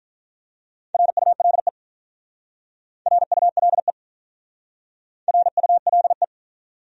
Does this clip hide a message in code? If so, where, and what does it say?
Yes: Morse code from 0.9 to 6.2 s, heard as a keyed tone, about 47 wpm, 700 Hz. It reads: RULE RULE RULE